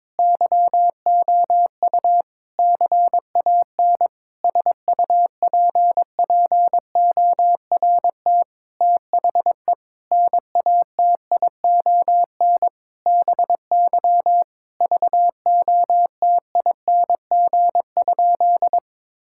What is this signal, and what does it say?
Morse code, heard as a keyed tone, about 22 wpm, 700 Hz. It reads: YOU CAN SUPPORT THE NATION BY VOTING?
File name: Sentences with 9-Word Phrases - 22wpm